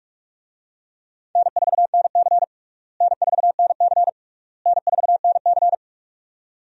Morse code, heard as a keyed tone, about 43 wpm, 700 Hz. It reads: N4NC N4NC N4NC
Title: USA Call Signs - 4 Characters - Repeated 3x - 45wpm